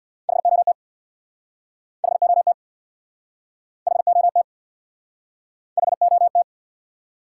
Morse code, heard as a keyed tone, about 61 wpm, 700 Hz. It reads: HOT HOT HOT HOT
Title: Speed-Racing Top 180 QSO Elements 50wpm